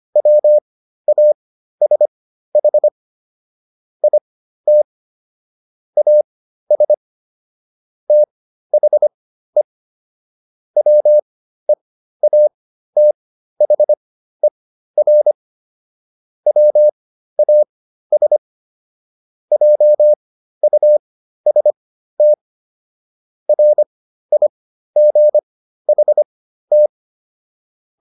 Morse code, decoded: WASH IT AS THE WEATHER WAS JUST RIGHT

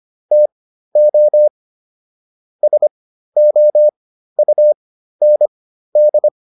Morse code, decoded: TO SOUND

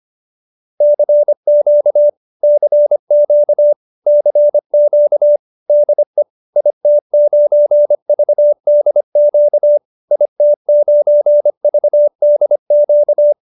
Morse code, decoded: CQ CQ CQ DE IT9VDQ IT9VDQ

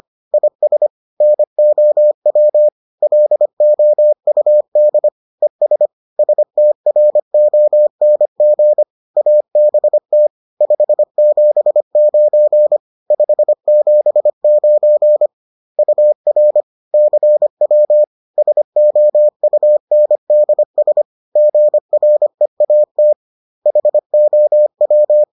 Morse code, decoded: IS NOW LOUD ES STRONG ABT 579 579 UR CW SOUNDS GREAT HOW

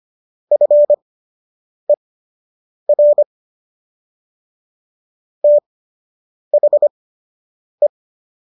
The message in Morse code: FER THE

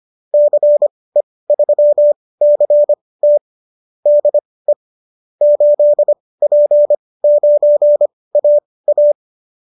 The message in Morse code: CE3CT DE 8P9AA

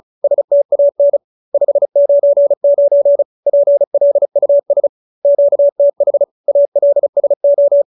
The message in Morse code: STAN 599 PLUS QTH ALSO

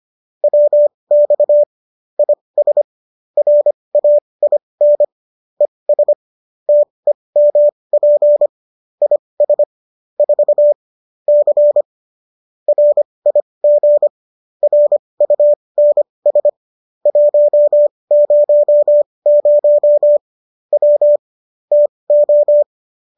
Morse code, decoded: WX IS RAIN ES TEMP IS 4 C RIG RUNS 100 W TO

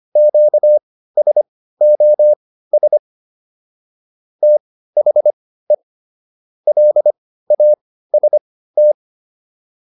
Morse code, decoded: QSOS THE LAST

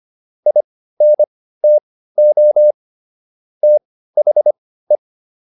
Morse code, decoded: INTO THE

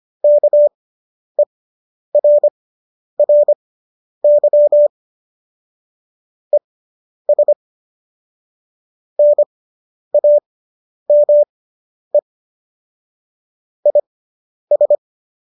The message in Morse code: KERRY ES NAME IS

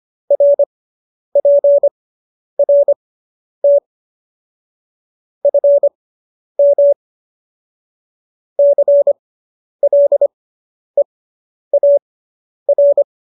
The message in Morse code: RPRT FM CLEAR

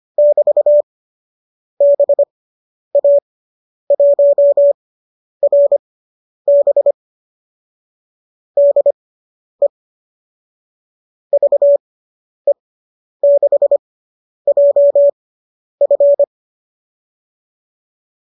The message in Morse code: <BT> BA1RB DE VE6JF

